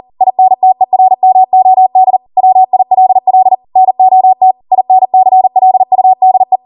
Morse code, heard as a keyed tone, about 40 wpm, 750 Hz. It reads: INTERMOD WILL NOT INCLUDE
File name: Morse / 40WPM